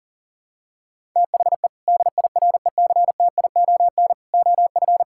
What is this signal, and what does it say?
Morse code, heard as a keyed tone, about 40 wpm, 700 Hz. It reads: THE DIRECTION OF
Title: Sentences from Top 1000 Words - Review Part 5 - 40wpm